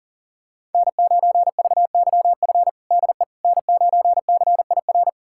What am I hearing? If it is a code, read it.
Morse code, heard as a keyed tone, about 40 wpm, 700 Hz. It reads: N9VYF DE N9CIR
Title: USA Callsign DE Callsign - 5 Characters - 40wpm